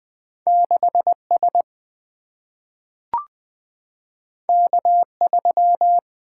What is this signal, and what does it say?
Morse code, heard as a keyed tone, about 20 wpm, 700 Hz. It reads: 6S K3
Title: Single-Letter Single-Number 20wpm